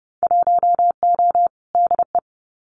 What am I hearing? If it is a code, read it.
Morse code, heard as a keyed tone, about 31 wpm, 700 Hz. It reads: 1O DE
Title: Winter Field Day Exchanges 30wpm